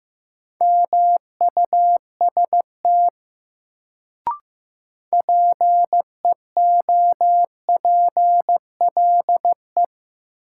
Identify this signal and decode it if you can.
Morse code, heard as a keyed tone, about 15 wpm, 700 Hz. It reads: MUST PEOPLE
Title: Sets of 2 Words - New Word Drill - 15wpm